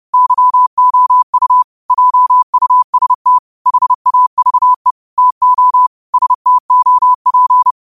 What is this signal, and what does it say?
Morse code, heard as a keyed tone, about 30 wpm, 1000 Hz. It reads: YOU JUST HAVE TO STOP
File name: qotdfiles / 30wpm